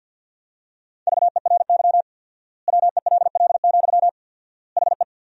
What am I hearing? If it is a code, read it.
Morse code, heard as a keyed tone, about 48 wpm, 700 Hz. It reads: VERY WELL, HE
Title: 4x Word-Spacing - Sentences from Top 200 Words - 50wpm